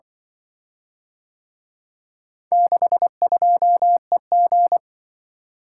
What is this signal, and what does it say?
Morse code, heard as a keyed tone, about 24 wpm, 700 Hz. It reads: 62EG
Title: Speed-Racing - Four Letter-Number Combinations - 20wpm